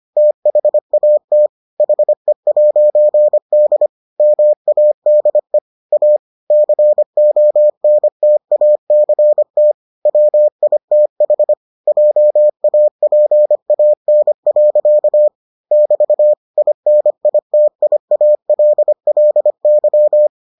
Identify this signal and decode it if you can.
Morse code, heard as a keyed tone, about 25 wpm, 600 Hz. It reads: THAT HE'D MADE A CONTACT WITH JAPAN. <BT> INITIALLY